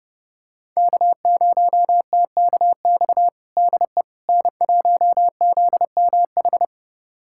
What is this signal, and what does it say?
Morse code, heard as a keyed tone, about 32 wpm, 700 Hz. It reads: K0TKX DE N1ZMH
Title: USA Callsign DE Callsign - 5 Characters - 30wpm